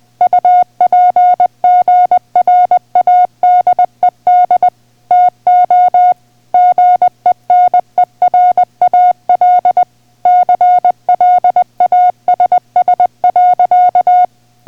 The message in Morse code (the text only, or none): UPGRADED TO GENERAL CLASS.